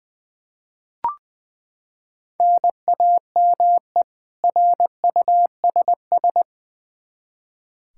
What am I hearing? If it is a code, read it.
Morse code, heard as a keyed tone, about 20 wpm, 700 Hz. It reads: NAME RUSS
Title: K1USN SST QSO Practice 20wpm